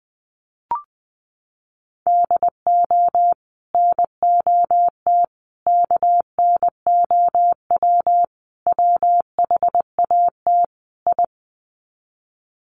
Morse code, heard as a keyed tone, about 20 wpm, 700 Hz. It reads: DO NOT KNOW WHAT I
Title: Sets of 5 Words 20wpm